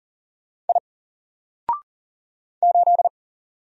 Morse code, heard as a keyed tone, about 40 wpm, 700 Hz. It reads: I 8
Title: Single Letter-Number 40wpm